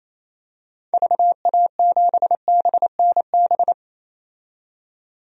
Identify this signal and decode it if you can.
Morse code, heard as a keyed tone, about 28 wpm, 700 Hz. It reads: VA7BNB
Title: DXCC-100 Call Signs - 6 Characters - 28wpm